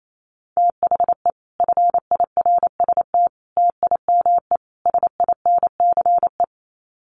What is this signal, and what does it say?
Morse code, heard as a keyed tone, about 28 wpm, 700 Hz. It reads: THE FIRST TIME SINCE
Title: Sets of 4 Words 28wpm